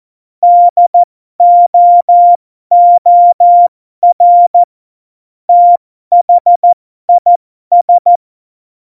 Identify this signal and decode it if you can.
Morse code, heard as a keyed tone, about 14 wpm, spaced as for 12 wpm, 700 Hz. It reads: DOOR THIS
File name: Morsekurs / Witze